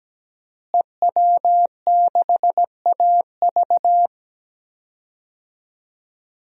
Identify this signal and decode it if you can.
Morse code, heard as a keyed tone, about 17 wpm, 700 Hz. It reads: EW6AV
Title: DXCC-100 Call Signs - 5 Characters - Repeated 2x - 17wpm